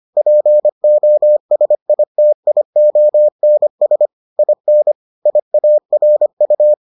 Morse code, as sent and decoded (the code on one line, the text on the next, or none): .--. --- ... .. - .. --- -. ... / .. -. / .. .- .-. ..-
POSITIONS IN IARU